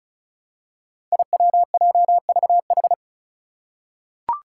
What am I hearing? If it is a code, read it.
Morse code, heard as a keyed tone, about 36 wpm, 700 Hz. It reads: IWJVH
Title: Five Challenging Letter-Number Combinations 35wpm